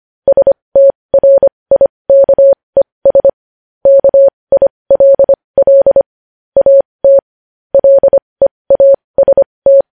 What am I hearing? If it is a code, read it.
Morse code, heard as a keyed tone, about 25 wpm, spaced as for 20 wpm, 550 Hz. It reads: STRIKES KILL AT LEAST
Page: news-headlines-in-morse-code-at-20-wpm